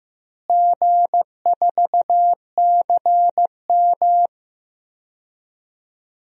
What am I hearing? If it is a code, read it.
Morse code, heard as a keyed tone, about 16 wpm, 700 Hz. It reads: G4CM
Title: DXCC-100 Call Signs - 4 Characters - Repeated 3x - 15wpm